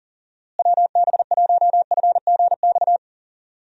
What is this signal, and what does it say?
Morse code, heard as a keyed tone, about 39 wpm, 700 Hz. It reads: WB1FGX
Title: USA Call Signs - 6 Characters - Repeated 3x - 40wpm